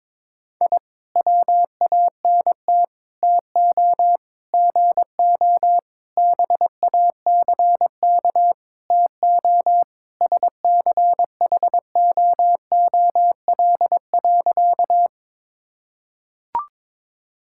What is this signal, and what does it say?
Morse code, heard as a keyed tone, about 21 wpm, 700 Hz. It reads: I WANT TO GO BACK TO SCHOOL.
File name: practice / 22wpm / Sentences with 4-Word Phrases - 22wpm